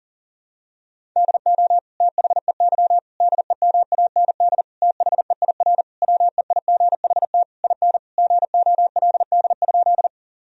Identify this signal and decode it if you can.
Morse code, heard as a keyed tone, about 40 wpm, 700 Hz. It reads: DO THEY DEMAND THEIR WEIGHT IN GOLD?
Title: Sentences from Top 1000 Words - Review Part 3 - 40wpm